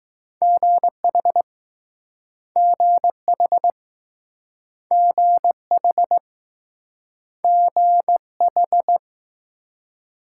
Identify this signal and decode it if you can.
Morse code, heard as a keyed tone, about 20 wpm, 700 Hz. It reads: GH GH GH GH